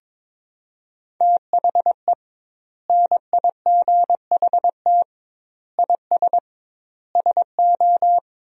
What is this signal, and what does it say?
Morse code, heard as a keyed tone, about 22 wpm, 700 Hz. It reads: THE NIGHT IS SO
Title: 2x Word-Spacing - Sentences from Top 500 Words - 22wpm